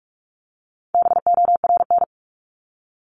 Morse code, heard as a keyed tone, about 44 wpm, 700 Hz. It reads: BORN
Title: Top 751-1000 Words 45wpm